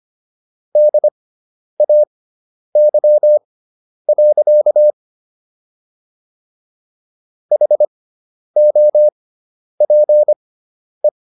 Morse code, heard as a keyed tone, about 25 wpm, spaced as for 10 wpm, 600 Hz. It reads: DAY. HOPE